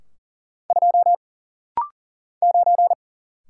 Morse code, heard as a keyed tone, about 41 wpm, 700 Hz. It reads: 2 9
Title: Single Letter-Number - Rapid-Fire 40wpm